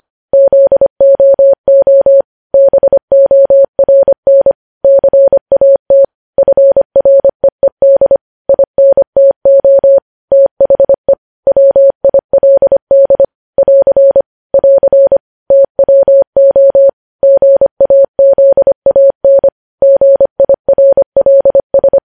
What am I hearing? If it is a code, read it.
Morse code, heard as a keyed tone, about 25 wpm, 550 Hz. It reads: ZOO BORN CAT FREED INTO THE WILD <AR> <AR> TWO GAZAN GIRLS
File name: podcast / News headlines-1x-25x25